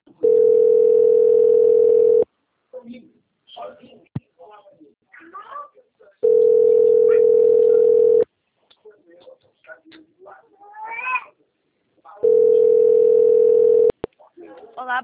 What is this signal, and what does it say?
Ringback tone (North America), 440 + 480 Hz, with speech.
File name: out- - - - - .39151